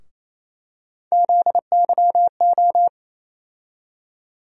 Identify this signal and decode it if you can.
Morse code, heard as a keyed tone, about 28 wpm, 700 Hz. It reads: ZYO
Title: USA Call Sign Postfixes - 3 Letters 28wpm